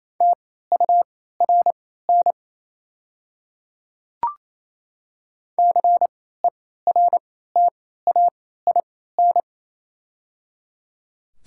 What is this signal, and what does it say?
Morse code, heard as a keyed tone, about 27 wpm, spaced as for 16 wpm, 690 Hz. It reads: TURN CERTAIN
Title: 3x Character Spacing - Top 500 Words - 28wpm